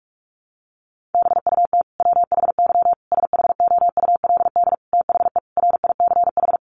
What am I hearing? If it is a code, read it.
Morse code, heard as a keyed tone, about 44 wpm, 700 Hz. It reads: BUT WHY SHOULD THE RICH
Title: Sentences from Top 1100 Words 45wpm